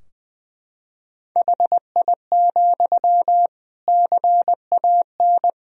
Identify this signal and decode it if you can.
Morse code, heard as a keyed tone, about 20 wpm, 700 Hz. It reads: HI, CAN
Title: Sentences from Top 1000 Words - Review Part 3 - 20wpm